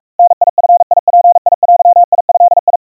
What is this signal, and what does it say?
Morse code (Japanese Wabun code), heard as a keyed tone, about 44 wpm, 700 Hz. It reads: ダヂヅデド
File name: 150 cpm 700 Hz Crispy